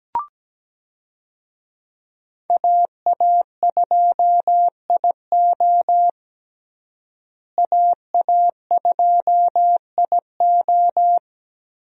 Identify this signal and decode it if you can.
Morse code, heard as a keyed tone, about 17 wpm, 700 Hz. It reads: AA2IO AA2IO
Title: USA Call Signs - 5 Characters - Repeated 3x - 17wpm